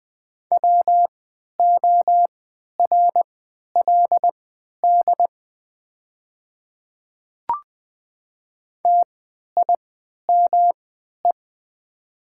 Morse code, heard as a keyed tone, about 20 wpm, spaced as for 11 wpm, 700 Hz. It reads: WORLD TIME
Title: 3x Character Spacing - Top 200 Words - 20wpm